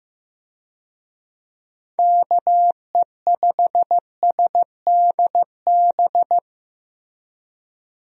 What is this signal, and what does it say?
Morse code, heard as a keyed tone, about 15 wpm, 700 Hz. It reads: KE5SDB
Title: USA Call Signs - 6 Characters - Repeated 2x - 15wpm